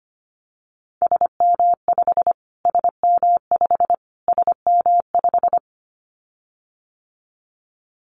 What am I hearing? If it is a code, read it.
Morse code, heard as a keyed tone, about 25 wpm, 700 Hz. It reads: SM5 SM5 SM5
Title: Callsign Prefixes for Top 25 DXCC - Repeated 3x - 25wpm